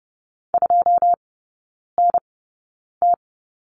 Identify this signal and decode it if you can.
Morse code, heard as a keyed tone, about 32 wpm, spaced as for 9 wpm, 700 Hz. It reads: 2NT
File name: 081 Lesson 14e - Number 2 - Three Characters 30wpm